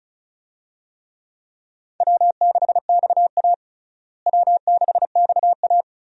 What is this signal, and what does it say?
Morse code, heard as a keyed tone, about 36 wpm, 700 Hz. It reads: W6XA W6XA
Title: USA Call Signs - 4 Characters - Repeated 3x - 35wpm